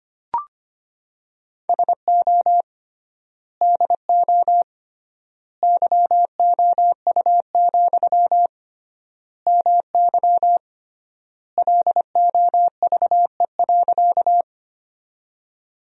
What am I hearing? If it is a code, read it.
Morse code, heard as a keyed tone, about 25 wpm, 700 Hz. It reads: SO DO YOU, MY LOVE.
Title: 3x Word-Spacing - Sentences from Top 400 Words - 25wpm